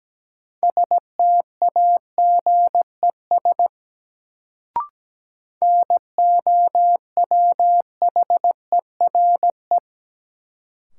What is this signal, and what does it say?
Morse code, heard as a keyed tone, about 17 wpm, 700 Hz. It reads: STAGES NOWHERE
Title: Sets of 5 Words Encore - New Word Drill - 17wpm